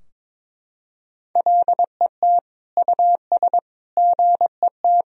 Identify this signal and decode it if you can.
Morse code, heard as a keyed tone, about 21 wpm, 700 Hz. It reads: LET US GET
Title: Sentences from Top 1000 Words - Review Part 4 - 22wpm